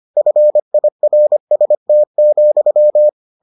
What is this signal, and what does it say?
Morse code, heard as a keyed tone, about 25 wpm, 600 Hz. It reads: FIRST,